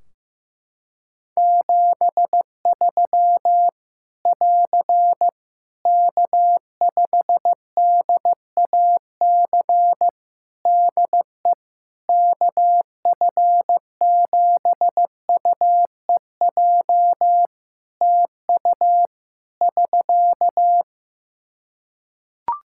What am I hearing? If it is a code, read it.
Morse code, heard as a keyed tone, about 15 wpm, 700 Hz. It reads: 73 <AR> K5DAC DE KF7UEJ TU <SK>
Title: Practice QSO - Exchange 4 - Ending - 15wpm